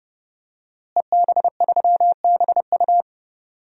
Morse code, heard as a keyed tone, about 32 wpm, 700 Hz. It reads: EB3BU
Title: DXCC-100 Call Signs - 5 Characters - Repeated 2x - 30wpm